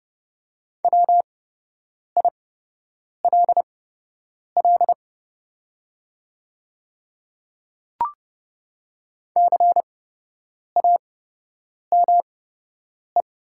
Morse code, heard as a keyed tone, about 30 wpm, spaced as for 8 wpm, 700 Hz. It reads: WILL CAME
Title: 8x Character Spacing - Top 300 Words - 30wpm